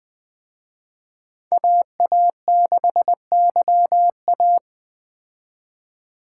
Morse code, heard as a keyed tone, about 20 wpm, 700 Hz. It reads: AA6YA
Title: Speed Racing - USA 2x2 Call Signs 17wpm